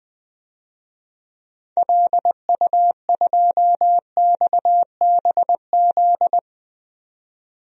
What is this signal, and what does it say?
Morse code, heard as a keyed tone, about 20 wpm, 700 Hz. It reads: LU2XBZ
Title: DXCC-100 Call Signs - 6 Characters - 20wpm